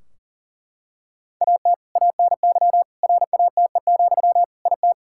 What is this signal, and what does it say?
Morse code, heard as a keyed tone, about 40 wpm, 700 Hz. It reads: AT ANY RATE, IT
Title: Sentences from Top 1600 Words 40wpm